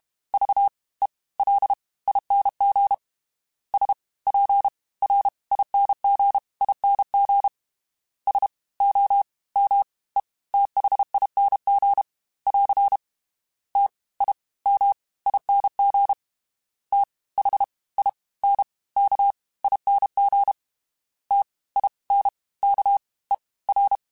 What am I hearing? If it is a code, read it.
Morse code, heard as a keyed tone, about 32 wpm, 800 Hz. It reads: V E L ING S P R ING ING S O M E THING <AR> T I M ING T H I N K ING T I N K E R
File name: Special INGs / Special INGs-mp3s / Fast ing @30